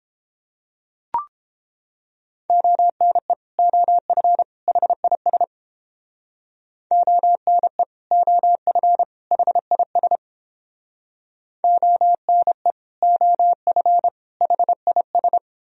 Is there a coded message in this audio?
Yes: Morse code from 2.5 to 15.4 s, heard as a keyed tone, about 30 wpm, 700 Hz. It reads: ONE OF HIS ONE OF HIS ONE OF HIS